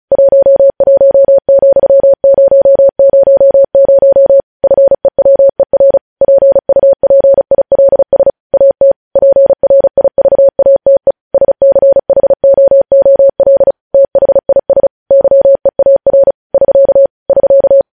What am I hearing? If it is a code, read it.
Morse code, heard as a keyed tone, about 36 wpm, 550 Hz. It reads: 11,000 FEWER PUPILS AT PRIVATE SCHOOL THIS YEAR <SK> <SK>
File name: podcast / News headlines-1x-35x35